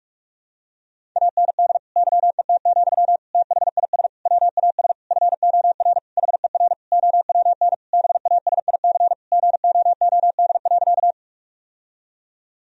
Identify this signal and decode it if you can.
Morse code, heard as a keyed tone, about 46 wpm, 700 Hz. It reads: AND YET, THIS WAS FOR HER OWN BASIC GOOD.
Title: Sentences from Top 1000 Words - Review Part 3 - 45wpm